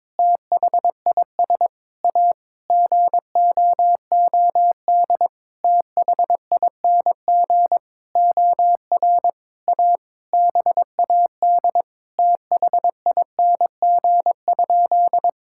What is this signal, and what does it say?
Morse code, heard as a keyed tone, about 22 wpm, 700 Hz. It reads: THIS A GOOD THING OR A BAD THING?
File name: Sentences with 7-Word Phrases - 22wpm